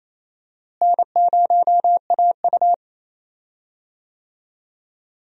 Morse code, heard as a keyed tone, about 28 wpm, 700 Hz. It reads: N0AU
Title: USA Call Signs - 4 Characters - Repeated 3x - 28wpm